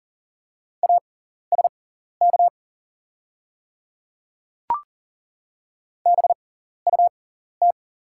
Morse code, heard as a keyed tone, about 40 wpm, spaced as for 14 wpm, 700 Hz. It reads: ASK BUT
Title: 6x Character Spacing - Top 200 Words - 40wpm